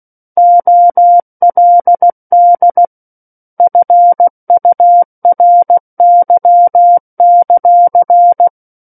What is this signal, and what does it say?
Morse code, heard as a keyed tone, about 16 wpm, 700 Hz. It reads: OLD FURY;